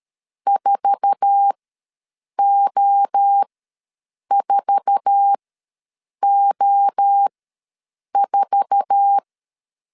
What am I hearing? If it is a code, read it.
Morse code, heard as a keyed tone, about 13 wpm, spaced as for 7 wpm, 800 Hz. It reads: 4O4O4
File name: LEZIONI / 8 YJ45 / 8 YJ4 ws 12wpm 3249